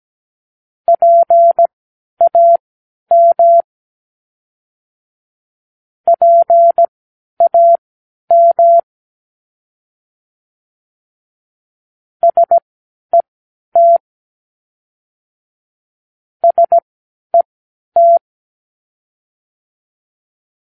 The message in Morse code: PAM PAM SET SET